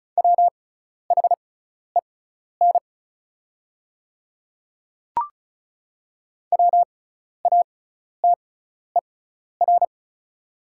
Morse code, heard as a keyed tone, about 36 wpm, spaced as for 12 wpm, 700 Hz. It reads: WHEN WATER
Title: 6x Character Spacing - Top 100 Words - 35wpm